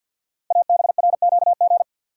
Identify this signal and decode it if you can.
Morse code, heard as a keyed tone, about 52 wpm, 700 Hz. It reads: ABRQG
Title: Code Groups - 5 Letters - 50wpm